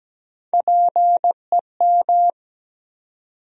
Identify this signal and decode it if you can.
Morse code, heard as a keyed tone, about 17 wpm, 700 Hz. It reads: PEM
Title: USA Call Sign Postfixes - 3 Letters 17wpm